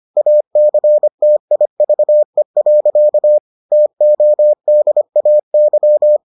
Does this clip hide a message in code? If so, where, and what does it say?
Yes: Morse code from 0.2 to 6.2 s, heard as a keyed tone, about 25 wpm, 600 Hz. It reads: ACTIVE. TODAY